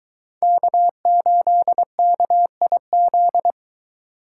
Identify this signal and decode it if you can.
Morse code, heard as a keyed tone, about 22 wpm, 700 Hz. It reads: K8KIZ